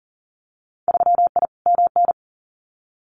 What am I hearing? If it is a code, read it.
Morse code, heard as a keyed tone, about 40 wpm, 700 Hz. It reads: 3I MN
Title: Winter Field Day Exchanges 40wpm